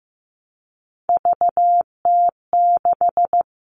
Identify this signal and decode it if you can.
Morse code, heard as a keyed tone, about 15 wpm, 700 Hz. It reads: VT6
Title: All Cut Numbers - 3-Digit 15wpm